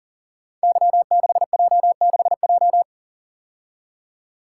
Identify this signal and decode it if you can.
Morse code, heard as a keyed tone, about 40 wpm, 700 Hz. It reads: Y6J6J
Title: Code Groups - 5 Letter:Numbers - 40wpm